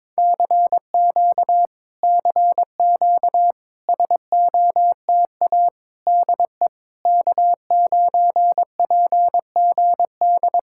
Morse code, heard as a keyed tone, about 22 wpm, 700 Hz. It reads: CQ CQ SOTA DE K9PGD
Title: Speed-Racing Common QSO Phrases - 22wpm